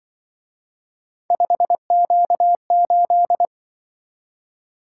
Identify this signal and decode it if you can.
Morse code, heard as a keyed tone, about 24 wpm, 700 Hz. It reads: 5Q8
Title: Speed-Racing - Three Letter-Number Combinations - 20wpm